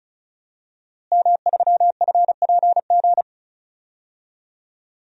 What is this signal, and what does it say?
Morse code, heard as a keyed tone, about 36 wpm, 700 Hz. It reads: M3FPG
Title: DXCC-100 Call Signs - 5 Characters - Repeated 3x - 35wpm